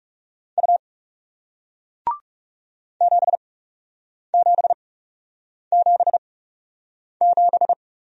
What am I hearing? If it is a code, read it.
Morse code, heard as a keyed tone, about 40 wpm, spaced as for 8 wpm, 700 Hz. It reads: U 7777